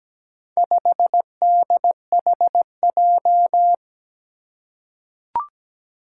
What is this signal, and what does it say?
Morse code, heard as a keyed tone, about 17 wpm, 700 Hz. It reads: 5DHJ
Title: Four Challenging Letter-Number Combinations 17wpm